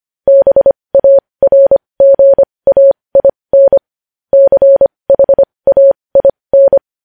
Morse code, heard as a keyed tone, about 25 wpm, spaced as for 20 wpm, 550 Hz. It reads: BARGAIN CHAIN
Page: News headlines in Morse code at 20 wpm